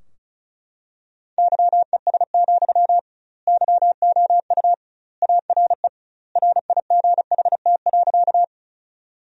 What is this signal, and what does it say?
Morse code, heard as a keyed tone, about 36 wpm, 700 Hz. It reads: YES, YOU ARE RIGHT.
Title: 2x Word-Spacing - Sentences from Top 500 Words - 35wpm